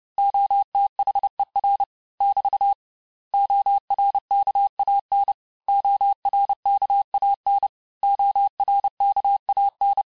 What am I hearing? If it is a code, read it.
Morse code, heard as a keyed tone, about 30 wpm, 800 Hz. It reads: OTHER <BT> ORKAN ORKAN ORKAN
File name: cours-du-mercredi / audios / cours-854-mots-vitesse-28-mpm